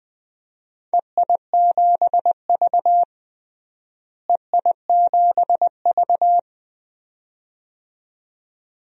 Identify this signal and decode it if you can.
Morse code, heard as a keyed tone, about 21 wpm, 700 Hz. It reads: EI7V EI7V
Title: DXCC-100 Call Signs - 4 Characters - Repeated 2x - 20wpm